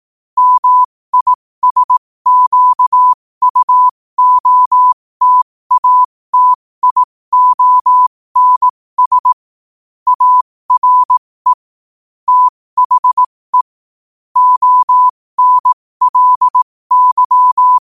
Morse code, heard as a keyed tone, about 18 wpm, spaced as for 15 wpm, 1000 Hz. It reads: MISQUOTATIONS ARE THE ONLY
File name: qotdfiles / 15wpm